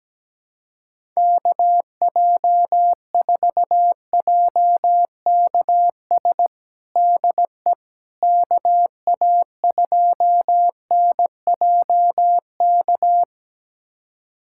Morse code, heard as a keyed tone, about 17 wpm, 700 Hz. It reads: KJ4JKS DE KA2NJK
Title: USA Callsign DE Callsign - 6 Characters - 17wpm